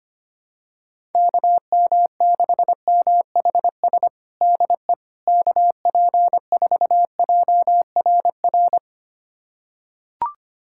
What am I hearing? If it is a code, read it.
Morse code, heard as a keyed tone, about 25 wpm, 700 Hz. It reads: KM6MHS DE KP4JRR
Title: USA Callsign DE Callsign - 6 Characters - 25wpm